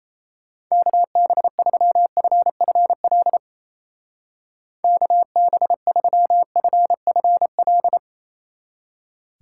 Morse code, heard as a keyed tone, about 33 wpm, 700 Hz. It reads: KB3FFL KB3FFL
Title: Speed Racing - USA 2x3 Call Signs 28wpm